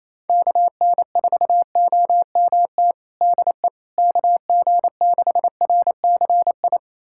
Morse code, heard as a keyed tone, about 28 wpm, 700 Hz. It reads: KN4OMT DE KG6RCI
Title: USA Callsign DE Callsign - 6 Characters - 28wpm